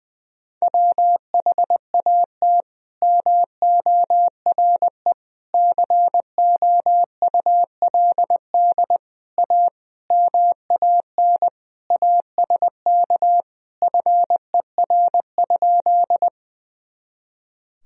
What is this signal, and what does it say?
Morse code, heard as a keyed tone, about 20 wpm, 700 Hz. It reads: WHAT MORE COULD A MAN ASK FER?
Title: Sentences from Top 200 Words and Abbreviations 20wpm